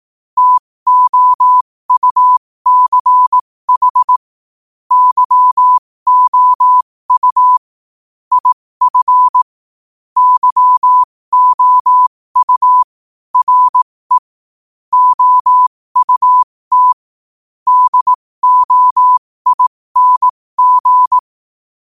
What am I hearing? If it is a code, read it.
Morse code, heard as a keyed tone, about 18 wpm, spaced as for 15 wpm, 1000 Hz. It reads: TOUCH YOU IF YOU RE OUT DOING
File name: qotdfiles / 15wpm